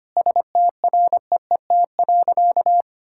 Morse code, heard as a keyed tone, about 25 wpm, 700 Hz. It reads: STREET.
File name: practice / 25wpm / Difficult Sentences with Hints - 25wpm